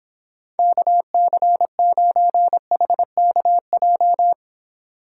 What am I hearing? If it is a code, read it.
Morse code, heard as a keyed tone, about 26 wpm, 700 Hz. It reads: KC9HKJ